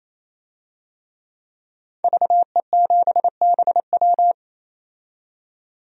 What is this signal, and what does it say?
Morse code, heard as a keyed tone, about 28 wpm, 700 Hz. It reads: VE7BW
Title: DXCC-100 Call Signs - 5 Characters - 28wpm